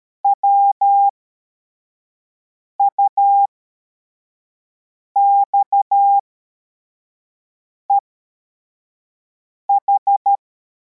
Morse code, heard as a keyed tone, about 13 wpm, spaced as for 4 wpm, 800 Hz. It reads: WUXEH
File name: formation / cw / Text73